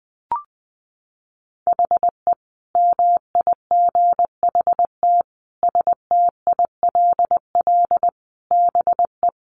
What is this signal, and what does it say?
Morse code, heard as a keyed tone, about 20 wpm, 700 Hz. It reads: HE MIGHT STILL BE
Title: Sentences from Top 300 Words 20wpm